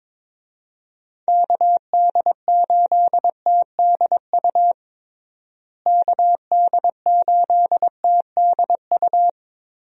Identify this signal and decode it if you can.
Morse code, heard as a keyed tone, about 22 wpm, 700 Hz. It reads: KD8TDU KD8TDU
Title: USA Call Signs - 6 Characters - Repeated 3x - 22wpm